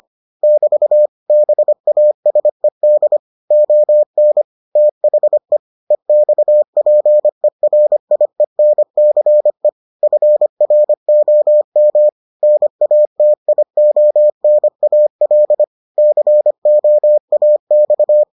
Morse code, heard as a keyed tone, about 25 wpm, 600 Hz. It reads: <BT> BASED ON THE EXPERIENCE FROM NATIONAL COAX